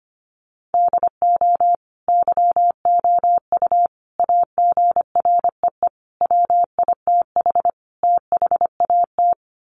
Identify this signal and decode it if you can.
Morse code, heard as a keyed tone, about 25 wpm, 700 Hz. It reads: DO YOU AGREE WITH THAT
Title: Sets of 5 Words Encore 25wpm